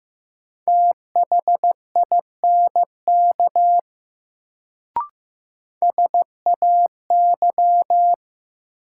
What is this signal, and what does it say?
Morse code, heard as a keyed tone, about 15 wpm, 700 Hz. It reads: THINK SAY
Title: Sets of 2 Words - New Word Drill - 15wpm